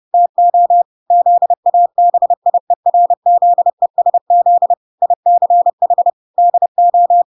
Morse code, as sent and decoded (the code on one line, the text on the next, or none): - --- / --.. .- -... .. .-. --.. ... --.. / .. -.-. .... / -.. ---
TO ZABIERZESZ ICH DO